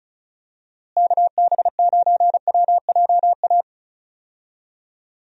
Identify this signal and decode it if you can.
Morse code, heard as a keyed tone, about 36 wpm, 700 Hz. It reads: KB9WJA
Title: USA Call Signs - 6 Characters - Repeated 3x - 35wpm